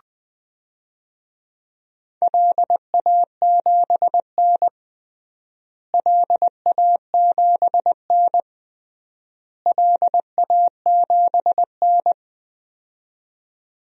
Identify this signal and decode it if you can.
Morse code, heard as a keyed tone, about 21 wpm, 700 Hz. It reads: LA7N LA7N LA7N
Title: DXCC-100 Call Signs - 4 Characters - Repeated 3x - 20wpm